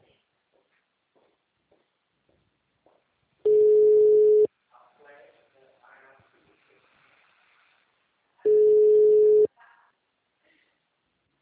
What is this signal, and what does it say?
Ringback tone (Europe, 425 Hz), with speech.